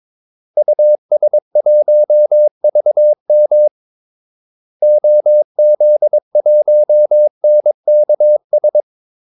Morse code, heard as a keyed tone, about 22 wpm, 600 Hz. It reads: US1VM OZ1NKS
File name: Callsigns-22wpm